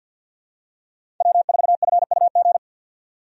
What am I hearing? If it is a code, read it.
Morse code, heard as a keyed tone, about 50 wpm, 700 Hz. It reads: W4FUG
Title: USA Call Signs - 5 Characters - 50wpm